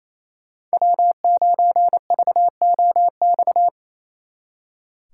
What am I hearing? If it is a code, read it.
Morse code, heard as a keyed tone, about 36 wpm, 700 Hz. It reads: W9VOX